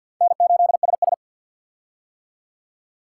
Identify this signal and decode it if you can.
Morse code, heard as a keyed tone, about 51 wpm, 700 Hz. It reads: N8SS
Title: K1USN SST QSO Practice - 50wpm